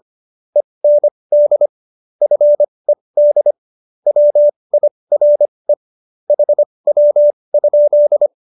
Morse code, heard as a keyed tone, about 25 wpm, spaced as for 20 wpm, 590 Hz. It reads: END FED WIRE HW?
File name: QSO313 20